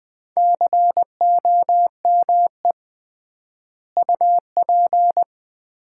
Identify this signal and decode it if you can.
Morse code, heard as a keyed tone, about 21 wpm, 700 Hz. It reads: COME UP
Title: 3x Word-Spacing - Sets of 3 Words - 20wpm